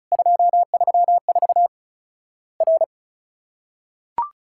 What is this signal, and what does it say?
Morse code, heard as a keyed tone, about 36 wpm, 700 Hz. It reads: 234 R
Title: Worked All Europe DX Contest - QTC Exchange - 35wpm